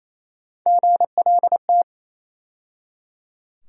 Morse code, heard as a keyed tone, about 28 wpm, 700 Hz. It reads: GLT
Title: USA Call Sign Postfixes - 3 Letters 28wpm